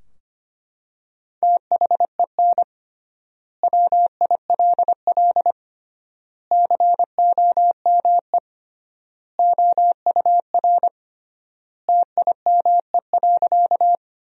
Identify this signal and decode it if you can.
Morse code, heard as a keyed tone, about 25 wpm, 700 Hz. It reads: THEN WILL COME OUR TIME.
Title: 3x Word-Spacing - Sentences from Top 200 Words - 25wpm